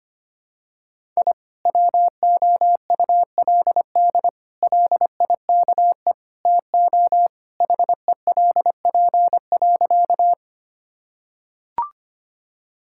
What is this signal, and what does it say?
Morse code, heard as a keyed tone, about 25 wpm, 700 Hz. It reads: I WOULD LIKE TO HELP.
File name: practice / 25wpm / Sentences with 4-Word Phrases - 25wpm